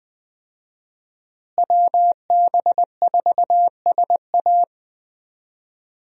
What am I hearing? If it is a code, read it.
Morse code, heard as a keyed tone, about 21 wpm, 700 Hz. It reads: WB4SA
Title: USA Call Signs - 5 Characters - Repeated 2x - 20wpm